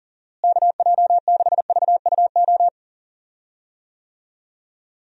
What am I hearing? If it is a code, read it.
Morse code, heard as a keyed tone, about 38 wpm, 700 Hz. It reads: KJ6VUO
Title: USA Call Signs - 6 Characters - Repeated 3x - 40wpm